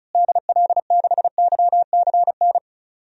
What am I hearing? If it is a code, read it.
Morse code, heard as a keyed tone, about 36 wpm, 700 Hz. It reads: DL6YCN
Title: DXCC-100 Call Signs - 6 Characters - Repeated 3x - 35wpm